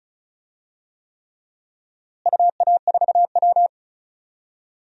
Morse code, heard as a keyed tone, about 36 wpm, 700 Hz. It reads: UA4W
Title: DXCC-100 Call Signs - 4 Characters - Repeated 3x - 35wpm